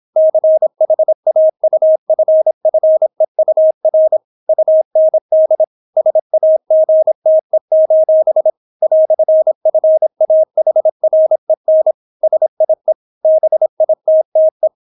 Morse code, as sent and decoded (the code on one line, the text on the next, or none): -.-. .... .- ..- ..-. ..-. ..- .-. / ..- -. -.. / ... .- --. - . ---... / .-..-. ..-. .- .... .-. -. / ... .. / -... .. - - .
CHAUFFEUR UND SAGTE: "FAHREN SIE BITTE